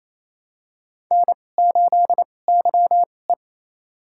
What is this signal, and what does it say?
Morse code, heard as a keyed tone, about 28 wpm, spaced as for 20 wpm, 700 Hz. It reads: N8YE
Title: USA Call Signs - 2x Character Spacing - 28wpm